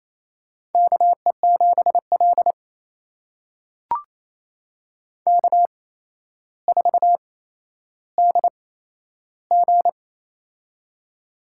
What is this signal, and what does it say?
Morse code, heard as a keyed tone, about 28 wpm, spaced as for 12 wpm, 700 Hz. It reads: KE7L K4DG
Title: USA Call Signs - 4 Characters - 8x Character Spacing - 28wpm